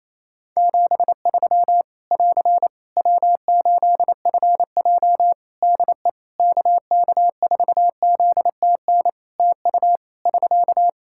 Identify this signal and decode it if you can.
Morse code, heard as a keyed tone, about 28 wpm, 700 Hz. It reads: 73 <AR> W8FJ DE KK4ZTN TU <SK>
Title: Practice QSO - Exchange 4 - Ending - 28wpm